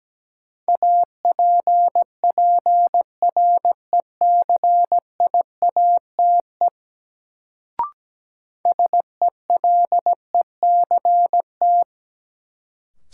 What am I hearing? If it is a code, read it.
Morse code, heard as a keyed tone, about 17 wpm, 700 Hz. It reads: APPRECIATE SELECT
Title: Sentences from Top 1300 Words - New Word Drill Alt - 17wpm